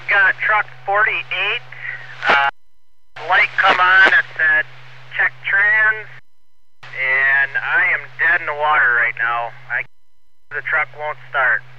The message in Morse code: none

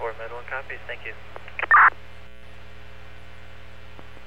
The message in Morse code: none

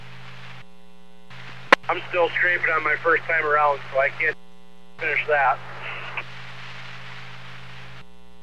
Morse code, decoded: none

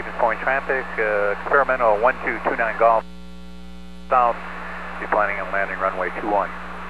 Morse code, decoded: none

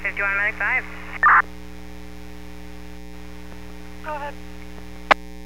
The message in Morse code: none

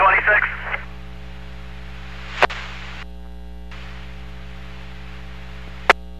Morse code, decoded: none

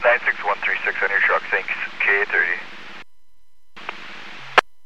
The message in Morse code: none

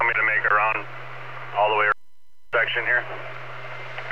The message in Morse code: none

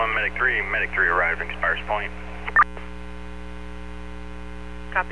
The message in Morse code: none

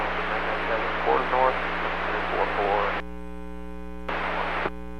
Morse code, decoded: none

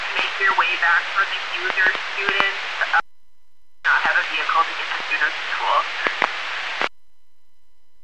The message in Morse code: none

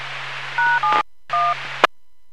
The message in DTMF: #71

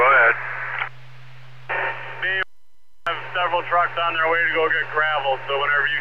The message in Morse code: none

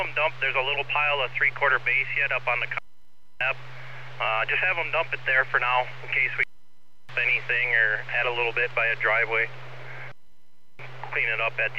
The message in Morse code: none

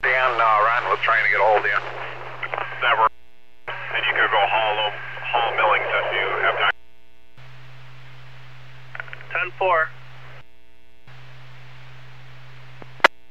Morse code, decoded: none

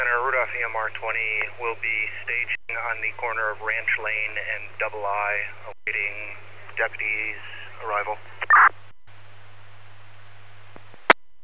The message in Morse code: none

